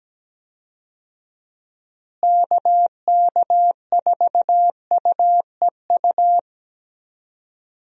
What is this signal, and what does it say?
Morse code, heard as a keyed tone, about 17 wpm, 700 Hz. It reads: KK4UEU
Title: USA Call Signs - 6 Characters - Repeated 2x - 17wpm